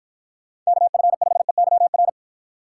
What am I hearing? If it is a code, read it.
Morse code, heard as a keyed tone, about 48 wpm, 700 Hz. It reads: KF5EYR